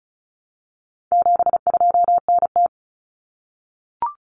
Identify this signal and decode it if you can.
Morse code, heard as a keyed tone, about 34 wpm, 700 Hz. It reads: 72NT
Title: Common Cut Numbers - 4-Digit 35wpm